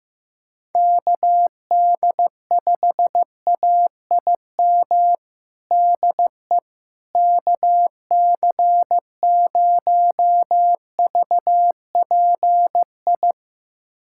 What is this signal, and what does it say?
Morse code, heard as a keyed tone, about 15 wpm, 700 Hz. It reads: KD5AIM DE KC0VPI
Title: USA Callsign DE Callsign 15wpm